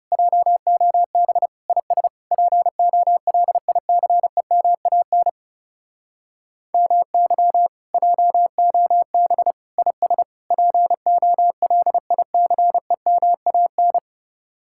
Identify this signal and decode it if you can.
Morse code, heard as a keyed tone, about 36 wpm, 700 Hz. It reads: JOB IS POLICEMAN MY JOB IS POLICEMAN
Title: Speed-Racing Common QSO Phrases - 30wpm